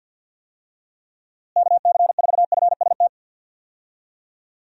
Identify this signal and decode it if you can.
Morse code, heard as a keyed tone, about 52 wpm, 700 Hz. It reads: KC4FST
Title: USA Call Signs - 6 Characters - Repeated 2x - 50wpm